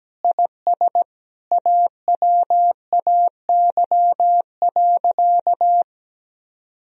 Morse code, heard as a keyed tone, about 17 wpm, 700 Hz. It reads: IS AWAY.
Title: Sentences from Top 1000 Words - Review Part 2 - 17wpm